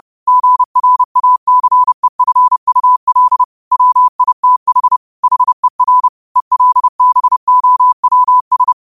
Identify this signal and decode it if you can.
Morse code, heard as a keyed tone, about 30 wpm, 1000 Hz. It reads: GRACEFUL WITH HER ELBOWS